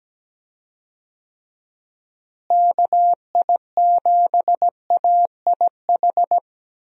Morse code, heard as a keyed tone, about 17 wpm, 700 Hz. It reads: KI7AIH
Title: USA Call Signs - 6 Characters - Repeated 3x - 17wpm